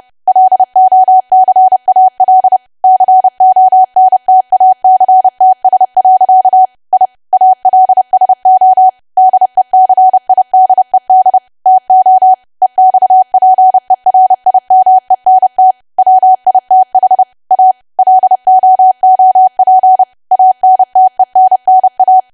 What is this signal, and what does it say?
Morse code, heard as a keyed tone, about 30 wpm, 750 Hz. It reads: LOCAL CONTACTS. I ALSO DECIDED TO EXPERIMENT WITH A LOOP ANTENNA
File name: cwa / W1AW-Level3 / 30WPM